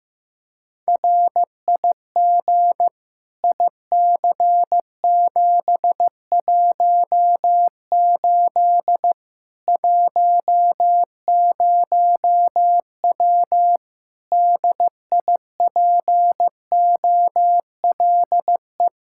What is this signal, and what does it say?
Morse code, heard as a keyed tone, about 15 wpm, 700 Hz. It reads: RIG IC718 10W DIPOLE
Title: Practice QSO - Exchange 2 - Info - 15wpm